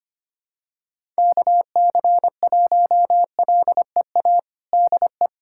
Morse code, heard as a keyed tone, about 25 wpm, 700 Hz. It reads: KC1LEA DE
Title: USA Callsign DE Callsign 25wpm